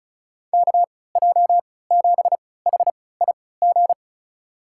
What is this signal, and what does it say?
Morse code, heard as a keyed tone, about 36 wpm, spaced as for 20 wpm, 700 Hz. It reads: KJ7HIG
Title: USA Call Signs - 3x Character Spacing - 35wpm